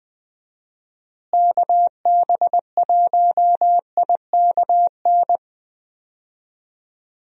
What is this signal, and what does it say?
Morse code, heard as a keyed tone, about 20 wpm, 700 Hz. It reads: KB1IKN
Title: USA Call Signs - 6 Characters - Repeated 2x - 20wpm